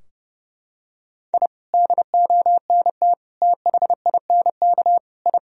Morse code, heard as a keyed tone, about 30 wpm, 700 Hz. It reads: I DONT THINK I
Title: Sentences from Top 1000 Words - Review Part 4 - 30wpm